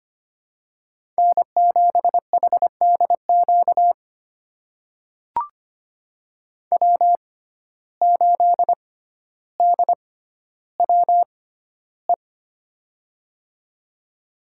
Morse code, heard as a keyed tone, about 25 wpm, spaced as for 13 wpm, 700 Hz. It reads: N7HDQ W8DWE
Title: USA Call Signs - 5 Characters - 6x Character Spacing - 25wpm